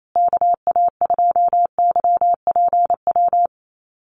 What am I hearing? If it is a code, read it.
Morse code, heard as a keyed tone, about 28 wpm, 700 Hz. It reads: KA2YPW
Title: USA Call Signs Repeated 3x 28wpm